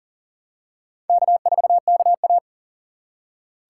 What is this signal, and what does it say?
Morse code, heard as a keyed tone, about 40 wpm, 700 Hz. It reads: K4KA
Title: USA Call Signs - 4 Characters - 4x Character Spacing - 40wpm